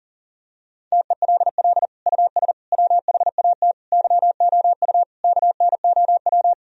Morse code, heard as a keyed tone, about 40 wpm, 700 Hz. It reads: TELL US WHAT YOU KNOW
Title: Sentences from Top 1000 Words - Review Part 4 - 40wpm